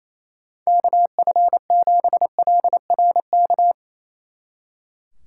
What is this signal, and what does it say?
Morse code, heard as a keyed tone, about 35 wpm, 700 Hz. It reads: KF7LRK
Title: Speed Racing - USA 2x3 Call Signs 28wpm